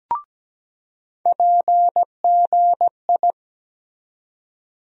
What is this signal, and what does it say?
Morse code, heard as a keyed tone, about 17 wpm, 700 Hz. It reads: PGI
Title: USA Call Sign Postfixes - 3 Letters 17wpm